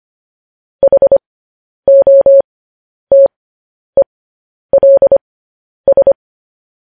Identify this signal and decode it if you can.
Morse code, heard as a keyed tone, about 25 wpm, spaced as for 10 wpm, 550 Hz. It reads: HOTELS
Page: news-headlines-in-morse-code-at-10-wpm